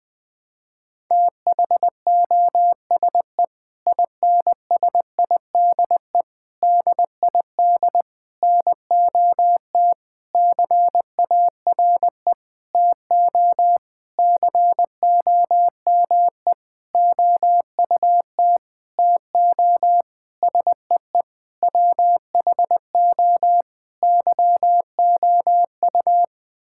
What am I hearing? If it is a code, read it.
Morse code, heard as a keyed tone, about 20 wpm, 700 Hz. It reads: THOSE INSIDE DID NOT CARE TO COME OUT TO SEE WHO YOU
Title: Sentences from Top 1000 Words - Review Part 6 - 20wpm